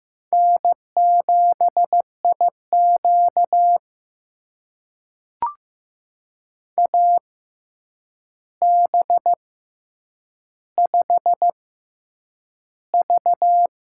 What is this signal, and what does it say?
Morse code, heard as a keyed tone, about 15 wpm, spaced as for 8 wpm, 700 Hz. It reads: N7IQ AB5V
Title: USA Call Signs - 4 Characters - 6x Character Spacing - 15wpm